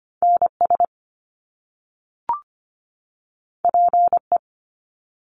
Morse code, heard as a keyed tone, about 25 wpm, 700 Hz. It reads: NS PE
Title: Canadian Provinces and Territories 25wpm